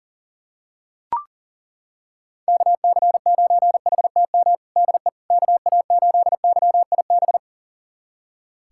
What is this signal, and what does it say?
Morse code, heard as a keyed tone, about 40 wpm, 700 Hz. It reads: KC9HTM DE KA8YIB
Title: USA Callsign DE Callsign 40wpm